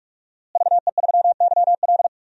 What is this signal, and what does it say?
Morse code, heard as a keyed tone, about 44 wpm, 700 Hz. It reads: VE3YL
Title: DXCC-100 Call Signs - 5 Characters - Repeated 3x - 45wpm